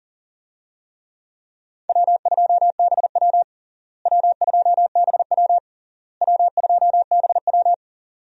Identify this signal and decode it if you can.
Morse code, heard as a keyed tone, about 37 wpm, 700 Hz. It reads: W2BW W2BW W2BW
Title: USA Call Signs - 4 Characters - Repeated 3x - 40wpm